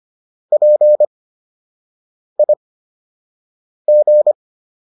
Morse code, heard as a keyed tone, about 25 wpm, spaced as for 6 wpm, 600 Hz. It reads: PIG